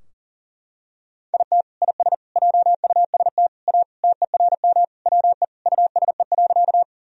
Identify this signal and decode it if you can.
Morse code, heard as a keyed tone, about 40 wpm, 700 Hz. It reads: IT IS JUST A TERM WE USE.
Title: Sentences from Top 1000 Words - Review Part 4 - 40wpm